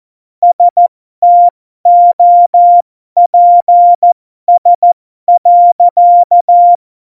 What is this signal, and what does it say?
Morse code, heard as a keyed tone, about 14 wpm, spaced as for 12 wpm, 700 Hz. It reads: STOPS.
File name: Morsekurs / Witze